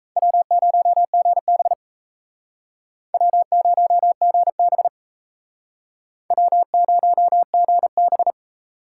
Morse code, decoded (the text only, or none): W0GB W0GB W0GB